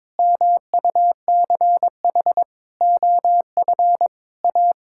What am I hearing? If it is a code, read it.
Morse code, heard as a keyed tone, about 22 wpm, 700 Hz. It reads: MUCH OF A
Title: Sentences from Top 1000 Words - Review Part 6 - 22wpm